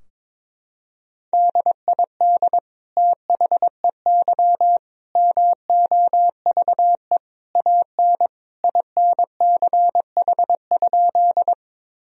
Morse code, heard as a keyed tone, about 22 wpm, 700 Hz. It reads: DID THEY MOVE AN INCH?
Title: Sentences from Top 1000 Words - Review Part 3 - 22wpm